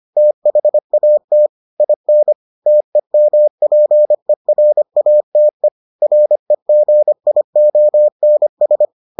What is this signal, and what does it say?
Morse code, heard as a keyed tone, about 25 wpm, 600 Hz. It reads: THAT IN TEMPERATE REGIONS